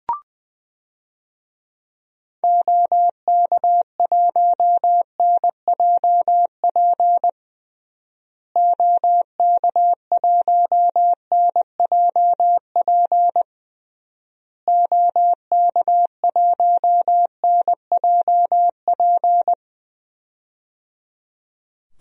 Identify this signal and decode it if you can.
Morse code, heard as a keyed tone, about 21 wpm, 700 Hz. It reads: OK1NJP OK1NJP OK1NJP
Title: DXCC-100 Call Signs - 6 Characters - Repeated 3x - 20wpm